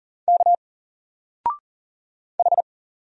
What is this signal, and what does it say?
Morse code, heard as a keyed tone, about 39 wpm, 700 Hz. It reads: K H
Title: Single Letter-Number 40wpm